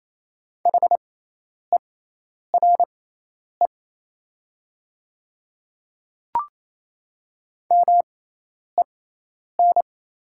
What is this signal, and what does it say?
Morse code, heard as a keyed tone, about 28 wpm, spaced as for 10 wpm, 700 Hz. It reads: HERE MEN
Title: 6x Character Spacing - Top 200 Words - 28wpm